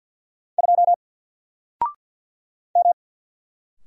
Morse code, heard as a keyed tone, about 48 wpm, 700 Hz. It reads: 2 M
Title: Single Letter-Number Alt 50wpm